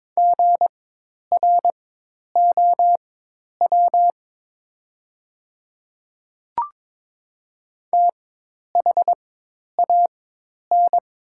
Morse code, heard as a keyed tone, about 22 wpm, spaced as for 10 wpm, 700 Hz. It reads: GROW THAN
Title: 4x Character Spacing - Top 500 Words - 22wpm